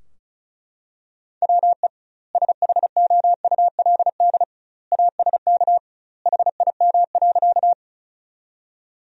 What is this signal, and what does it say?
Morse code, heard as a keyed tone, about 36 wpm, 700 Hz. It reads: WE SHOULD ASK HIM.
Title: 2x Word-Spacing - Sentences from Top 300 Words - 35wpm